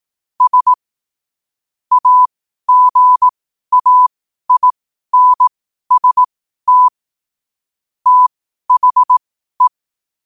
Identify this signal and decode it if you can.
Morse code, heard as a keyed tone, about 18 wpm, spaced as for 12 wpm, 1000 Hz. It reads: S AGAINST THE
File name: qotdfiles / 12wpm